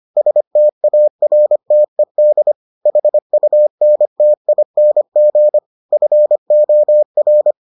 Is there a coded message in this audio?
Yes: Morse code from 0.2 to 7.5 s, heard as a keyed tone, about 25 wpm, 600 Hz. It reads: STARTED HUNTING FOR